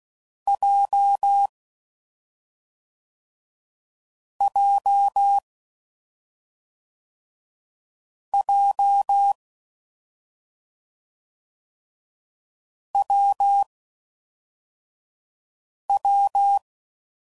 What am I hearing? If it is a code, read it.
Morse code, heard as a keyed tone, about 16 wpm, 800 Hz. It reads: J J J W W
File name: Combinations / Difficult Combos / AWJ1